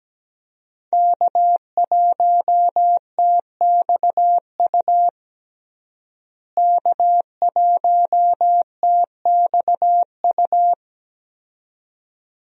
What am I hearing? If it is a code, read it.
Morse code, heard as a keyed tone, about 17 wpm, 700 Hz. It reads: K1TXU K1TXU
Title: USA Call Signs - 5 Characters - Repeated 2x - 17wpm